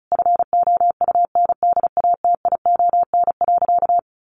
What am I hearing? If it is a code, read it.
Morse code, heard as a keyed tone, about 34 wpm, 700 Hz. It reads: FOUNDATION.
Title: Sentences from Top 1200 Words 35wpm